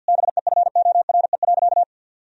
Morse code, heard as a keyed tone, about 50 wpm, 700 Hz. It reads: BEFORE.
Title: Sentences from Top 1000 Words - Review Part 3 - 50wpm